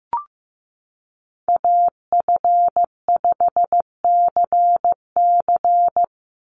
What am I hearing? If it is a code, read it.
Morse code, heard as a keyed tone, about 15 wpm, 700 Hz. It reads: AF5CC
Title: Call Signs Repeated 3x - 15wpm